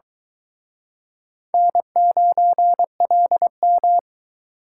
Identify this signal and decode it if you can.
Morse code, heard as a keyed tone, about 22 wpm, 700 Hz. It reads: N9LM